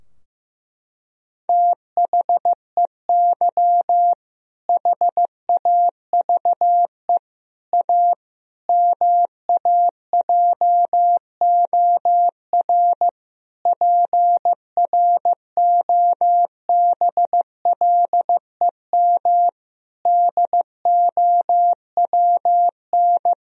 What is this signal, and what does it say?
Morse code, heard as a keyed tone, about 15 wpm, 700 Hz. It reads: THEY HAVE A MAJOR PROBLEM DOWN
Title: Sentences from Top 1000 Words - Review Part 4 - 15wpm